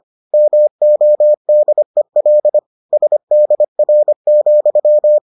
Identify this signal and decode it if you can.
Morse code, heard as a keyed tone, about 25 wpm, 600 Hz. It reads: MODEL SDR,